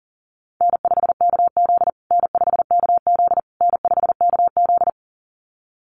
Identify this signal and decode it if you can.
Morse code, heard as a keyed tone, about 40 wpm, 700 Hz. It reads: N5KZ N5KZ N5KZ
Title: USA Call Signs Repeated 3x 40wpm